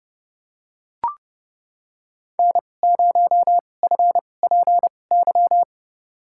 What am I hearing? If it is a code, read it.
Morse code, heard as a keyed tone, about 30 wpm, spaced as for 22 wpm, 700 Hz. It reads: N0FPY
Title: USA Call Signs - 2x Character Spacing - 30wpm